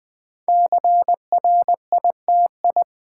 Morse code, heard as a keyed tone, about 20 wpm, 700 Hz. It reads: CRITI
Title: Build A Word - 5 Letters At A Time - 10-Letter Words - 20wpm